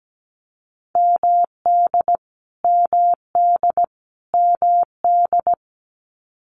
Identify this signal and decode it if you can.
Morse code, heard as a keyed tone, about 17 wpm, 700 Hz. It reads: MD MD MD
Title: US States Repeated 3x 17wpm